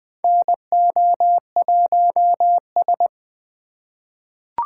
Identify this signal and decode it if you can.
Morse code, heard as a keyed tone, about 20 wpm, 700 Hz. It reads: NO1S
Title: USA Call Signs - 4 Characters - 4x Character Spacing - 20wpm